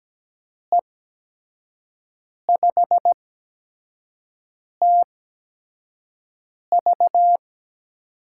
Morse code, heard as a keyed tone, about 17 wpm, spaced as for 5 wpm, 700 Hz. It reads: E5TV
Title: USA Call Signs - 6 Characters - 8x Character Spacing - 17wpm